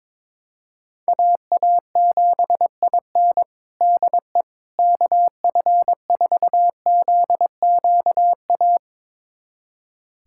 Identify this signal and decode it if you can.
Morse code, heard as a keyed tone, about 22 wpm, 700 Hz. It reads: AA7IN DE KF4ZQA
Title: USA Callsign DE Callsign 22wpm